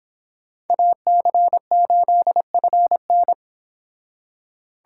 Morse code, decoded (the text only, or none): AC8FN